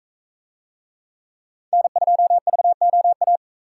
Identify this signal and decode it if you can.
Morse code, heard as a keyed tone, about 42 wpm, 700 Hz. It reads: N2VOA